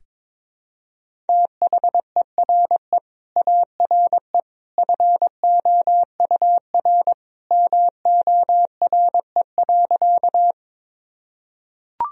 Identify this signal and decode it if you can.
Morse code, heard as a keyed tone, about 21 wpm, 700 Hz. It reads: THERE ARE FOUR MORE.
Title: Sentences from Top 1000 Words - Review Part 1 - 22wpm